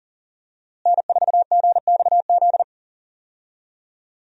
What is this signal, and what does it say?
Morse code, heard as a keyed tone, about 39 wpm, 700 Hz. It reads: N4GXZ
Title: USA Call Signs - 5 Characters - Repeated 3x - 40wpm